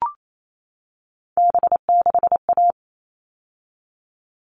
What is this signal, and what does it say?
Morse code, heard as a keyed tone, about 28 wpm, 700 Hz. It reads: B6A
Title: All Cut Numbers - 3-Digit 28wpm